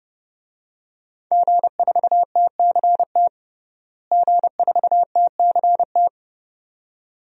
Morse code, heard as a keyed tone, about 30 wpm, 700 Hz. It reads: G4TCT G4TCT
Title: DXCC-100 Call Signs - 5 Characters - Repeated 2x - 30wpm